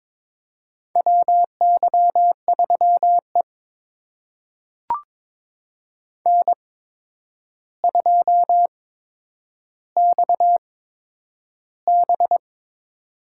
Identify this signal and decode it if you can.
Morse code, heard as a keyed tone, about 22 wpm, spaced as for 10 wpm, 700 Hz. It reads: WY3E N2XB
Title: USA Call Signs - 4 Characters - 8x Character Spacing - 22wpm